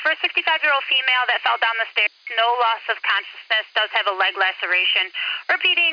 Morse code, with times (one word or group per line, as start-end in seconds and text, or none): none